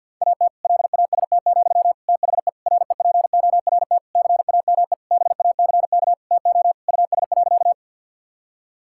Morse code, heard as a keyed tone, about 51 wpm, 700 Hz. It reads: AT LAST, THE REPORT CAME BACK TO US.
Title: Sentences from Top 1000 Words - Review Part 6 - 50wpm